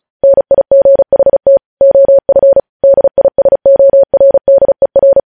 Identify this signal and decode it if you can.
Morse code, heard as a keyed tone, about 36 wpm, 550 Hz. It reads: NIGHT OF DISORDER
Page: news headlines in morse code at 35 wpm